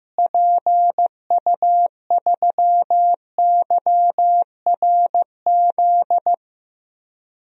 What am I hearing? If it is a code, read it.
Morse code, heard as a keyed tone, about 15 wpm, 700 Hz. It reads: PU3YRZ
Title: DXCC-100 Call Signs - 6 Characters - Repeated 3x - 15wpm